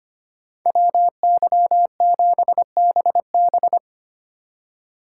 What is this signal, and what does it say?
Morse code, heard as a keyed tone, about 33 wpm, 700 Hz. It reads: WY7BB